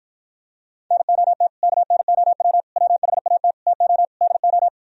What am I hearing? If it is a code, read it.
Morse code, heard as a keyed tone, about 43 wpm, 700 Hz. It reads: NOT KNOW WHAT TO DO
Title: Speed-Racing - Sets of 5 Words 35wpm